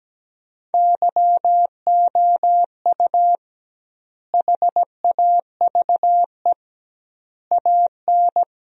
Morse code, heard as a keyed tone, about 17 wpm, 700 Hz. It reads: YOU HAVE AN
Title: 2x Word-Spacing - Sentences from Top 400 Words - 17wpm